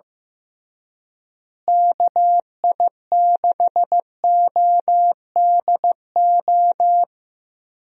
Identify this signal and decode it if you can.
Morse code, heard as a keyed tone, about 15 wpm, 700 Hz. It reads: KI6ODO
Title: USA Call Signs - 6 Characters - Repeated 3x - 15wpm